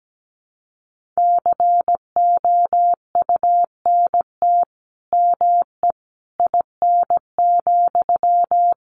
Morse code, heard as a keyed tone, about 17 wpm, 700 Hz. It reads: COUNT ME IN,
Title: Sentences from Top 1500 Words 17wpm